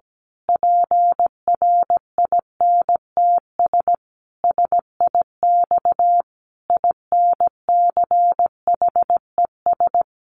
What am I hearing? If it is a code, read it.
Morse code, heard as a keyed tone, about 17 wpm, 700 Hz. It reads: PRINTS SIX INCHES
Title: A Sound Of Thunder 17wpm